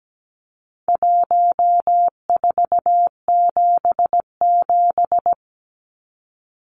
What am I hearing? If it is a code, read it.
Morse code, heard as a keyed tone, about 17 wpm, 700 Hz. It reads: 1477
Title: Numbers - Contest Style - 4-Digit 17wpm